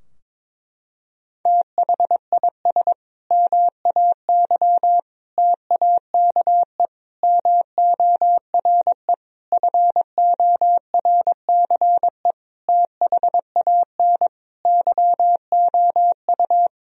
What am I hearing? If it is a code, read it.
Morse code, heard as a keyed tone, about 22 wpm, 700 Hz. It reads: THIS MAY TAKE MORE FORCE THAN YOU
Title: Sentences from Top 1000 Words - Review Part 3 - 22wpm